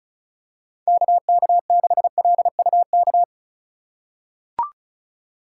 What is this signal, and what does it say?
Morse code, heard as a keyed tone, about 36 wpm, 700 Hz. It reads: KK6LUK
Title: USA Call Signs - 8x Character Spacing - 35wpm